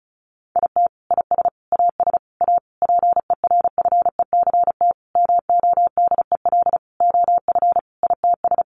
Morse code, heard as a keyed tone, about 34 wpm, 700 Hz. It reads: IT IS AS A PERFECT MODEL OF ITS
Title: Sentences from Top 1000 Words 35wpm